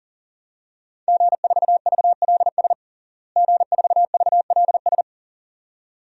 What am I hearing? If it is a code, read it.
Morse code, heard as a keyed tone, about 40 wpm, 700 Hz. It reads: G4VLS G4VLS
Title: DXCC-100 Call Signs - 5 Characters - Repeated 2x - 40wpm